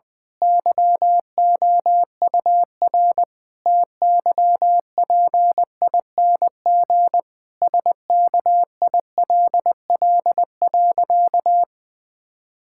Morse code, heard as a keyed tone, about 20 wpm, 700 Hz. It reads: YOUR TYPING SKILL.
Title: Pangram Challenge v2 - 20wpm